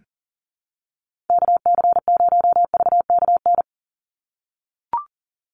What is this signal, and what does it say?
Morse code, heard as a keyed tone, about 38 wpm, 700 Hz. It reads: KC0VKN
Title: Call Signs - 40wpm